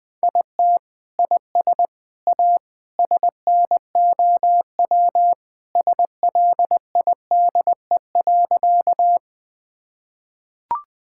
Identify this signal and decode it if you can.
Morse code, heard as a keyed tone, about 20 wpm, 700 Hz. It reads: IT IS A SNOW SLIDE.
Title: Sentences from Top 1600 Words 20wpm